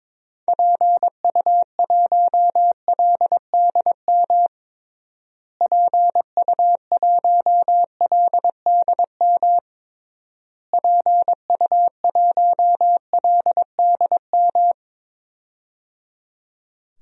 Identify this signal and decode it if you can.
Morse code, heard as a keyed tone, about 21 wpm, 700 Hz. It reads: PU1LDM PU1LDM PU1LDM
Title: DXCC-100 Call Signs - 6 Characters - Repeated 3x - 22wpm